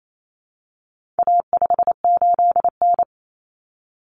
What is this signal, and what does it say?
Morse code, heard as a keyed tone, about 28 wpm, 700 Hz. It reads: A58N
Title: Common Cut Numbers - 4-Digit 28wpm